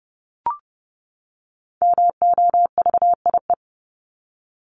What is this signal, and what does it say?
Morse code, heard as a keyed tone, about 30 wpm, 700 Hz. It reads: MOVIE
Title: Top 501-750 Words 30wpm